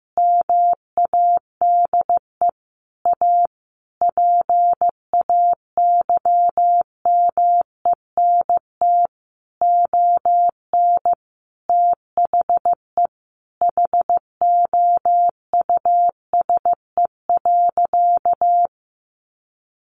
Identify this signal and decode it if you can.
Morse code, heard as a keyed tone, about 14 wpm, 700 Hz. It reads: MADE A PAYMENT ON THE HOUSE.
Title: Sentences from Top 1400 Words 15wpm